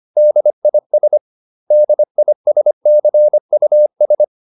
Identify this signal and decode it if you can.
Morse code, heard as a keyed tone, about 25 wpm, 600 Hz. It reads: DIS DISCUS